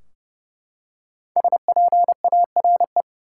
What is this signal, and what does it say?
Morse code, heard as a keyed tone, about 30 wpm, 700 Hz. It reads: SPARE
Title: Five Letter Words - 30wpm